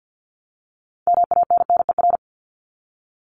Morse code, heard as a keyed tone, about 50 wpm, 700 Hz. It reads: MANNER
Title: Top 1501-1750 Words 50wpm